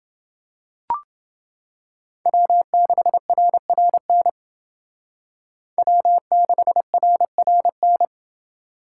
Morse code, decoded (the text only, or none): W6RRN W6RRN